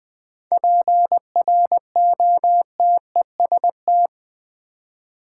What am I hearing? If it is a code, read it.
Morse code, heard as a keyed tone, about 20 wpm, 700 Hz. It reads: PROTEST
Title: 7 Letter Words - First 3 Letters Given - 20wpm